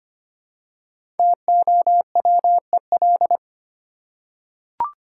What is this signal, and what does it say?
Morse code, heard as a keyed tone, about 25 wpm, 700 Hz. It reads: TOWEL
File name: practice / 25wpm / Easy Words with Hints - 25wpm